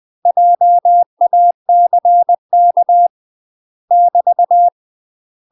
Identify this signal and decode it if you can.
Morse code, heard as a keyed tone, about 20 wpm, 700 Hz. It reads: JACK <BT>
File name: UK MASTER QSOs TRAINING / 20WPM / UK MASTER QSOs 0017